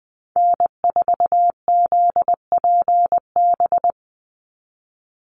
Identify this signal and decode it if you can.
Morse code, heard as a keyed tone, about 20 wpm, 700 Hz. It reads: N4ZPB
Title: USA Call Signs 20wpm